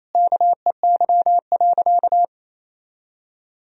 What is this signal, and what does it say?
Morse code, heard as a keyed tone, about 28 wpm, 700 Hz. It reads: KEY.
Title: Sentences from Top 1000 Words - Review Part 4 - 28wpm